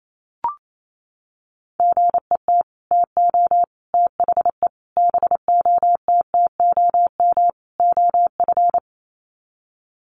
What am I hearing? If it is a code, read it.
Morse code, heard as a keyed tone, about 28 wpm, 700 Hz. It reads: GET TO THE BOTTOM OF
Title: Sets of 5 Words Encore 28wpm